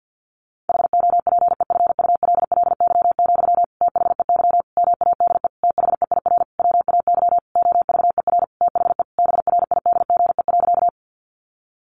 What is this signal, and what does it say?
Morse code, heard as a keyed tone, about 50 wpm, 700 Hz. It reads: HOPEFULLY, THEY MADE THEIR WAY OVER THE BRIDGE.
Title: Sentences from Top 1500 Words 50wpm